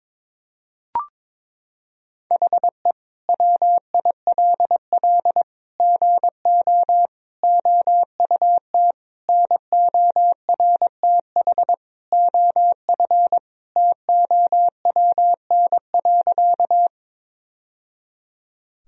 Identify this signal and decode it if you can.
Morse code, heard as a keyed tone, about 22 wpm, 700 Hz. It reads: HE WILL GO OUT NORTH OF TOWN.
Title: Sentences from Top 1000 Words - Review Part 1 - 22wpm